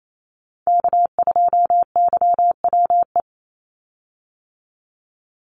Morse code, heard as a keyed tone, about 28 wpm, 700 Hz. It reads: K2YWE
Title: Call Signs - 28wpm